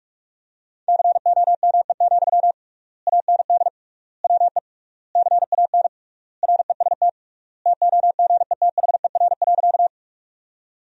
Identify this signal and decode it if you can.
Morse code, heard as a keyed tone, about 46 wpm, 700 Hz. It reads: COME, AND WE CAN REST TOGETHER.
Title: 3x Word-Spacing - Sentences from Top 500 Words - 45wpm